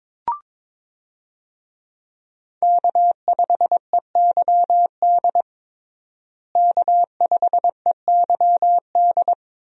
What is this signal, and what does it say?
Morse code, heard as a keyed tone, about 22 wpm, 700 Hz. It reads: K5EYD K5EYD
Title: USA Call Signs - 5 Characters - Repeated 2x - 22wpm